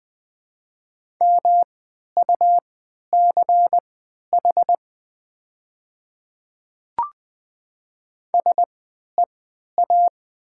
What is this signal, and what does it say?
Morse code, heard as a keyed tone, about 20 wpm, spaced as for 11 wpm, 700 Hz. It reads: MUCH SEA
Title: 3x Character Spacing - Top 300 Words - 20wpm